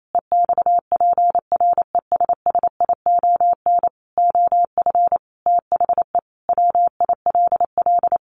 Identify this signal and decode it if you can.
Morse code, heard as a keyed tone, about 28 wpm, 700 Hz. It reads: EXPRESSION OF THE WILL
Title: Sentences from Top 1500 Words 28wpm